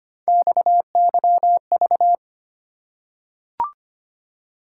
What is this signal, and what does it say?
Morse code, heard as a keyed tone, about 25 wpm, 700 Hz. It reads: XYV
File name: Three Letter-Number Combinations - 25wpm